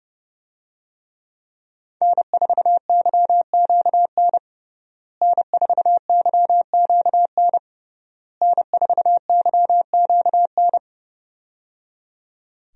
Morse code, heard as a keyed tone, about 30 wpm, 700 Hz. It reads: N4YQN N4YQN N4YQN
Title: USA Call Signs - 5 Characters - Repeated 3x - 30wpm